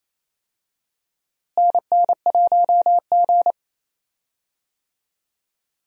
Morse code, heard as a keyed tone, about 28 wpm, 700 Hz. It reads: NN1G
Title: USA Call Signs - 4 Characters - Repeated 3x - 28wpm